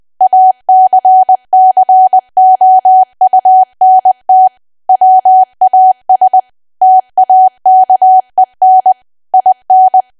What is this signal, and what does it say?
Morse code, heard as a keyed tone, about 20 wpm, 750 Hz. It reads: ACCOUNT WAS TAKEN IN
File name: cwa / W1AW-Level3 / 20WPM